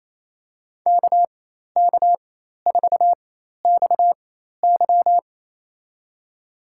Morse code, heard as a keyed tone, about 27 wpm, spaced as for 13 wpm, 700 Hz. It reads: KK4XY
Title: USA Call Signs - 5 Characters - 4x Character Spacing - 28wpm